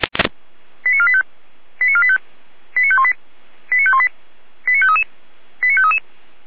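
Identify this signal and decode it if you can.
Five-tone selective call (ZVEI), heard as quick runs of five tones, tones of about 70 ms. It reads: 89485 89485 89518 89518 89533 89533
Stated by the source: ZVEI five-tone call received via SDR.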